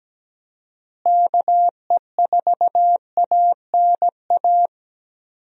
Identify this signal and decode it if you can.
Morse code, heard as a keyed tone, about 17 wpm, 700 Hz. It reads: KE4ANA
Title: USA Call Signs - 6 Characters - Repeated 3x - 17wpm